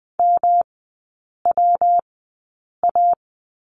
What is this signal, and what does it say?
Morse code, heard as a keyed tone, about 20 wpm, spaced as for 8 wpm, 700 Hz. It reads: MWA
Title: 114 Lesson 19e - Letter w - Three Characters 20wpm